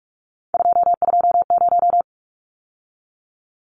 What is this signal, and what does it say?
Morse code, heard as a keyed tone, about 44 wpm, 700 Hz. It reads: 220
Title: Numbers - 3-Digit 45wpm